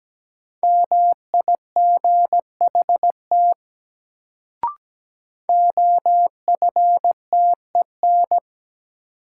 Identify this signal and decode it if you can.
Morse code, heard as a keyed tone, about 17 wpm, 700 Hz. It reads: MIGHT OFTEN
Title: Sentences from Top 300 Words - New Word Drill - 17wpm